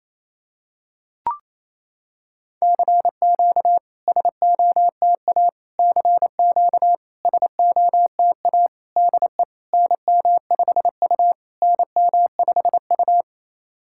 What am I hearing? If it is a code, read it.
Morse code, heard as a keyed tone, about 27 wpm, 690 Hz. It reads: CQ SOTA CQ SOTA DE NM5U NM5U
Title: SOTA QSO Practice 28wpm